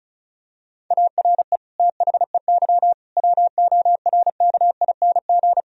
Morse code, heard as a keyed tone, about 36 wpm, 700 Hz. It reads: ARE THEY WORKING
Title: Sentences from Top 1000 Words - Review Part 4 - 35wpm